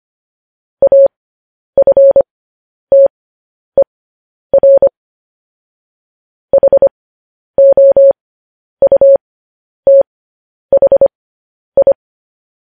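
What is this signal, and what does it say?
Morse code, heard as a keyed tone, about 25 wpm, spaced as for 10 wpm, 550 Hz. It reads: AFTER HOUTHI
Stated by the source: Morse code news headlines sent at 10 wpm.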